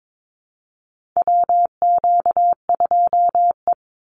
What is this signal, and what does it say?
Morse code, heard as a keyed tone, about 22 wpm, 700 Hz. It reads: WQ2E
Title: Call Signs Repeated 2x - 22wpm